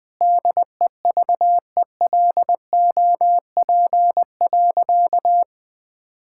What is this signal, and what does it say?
Morse code, heard as a keyed tone, about 20 wpm, 700 Hz. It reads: DEVELOP.
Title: Sentences with 6-Word Phrases - 20wpm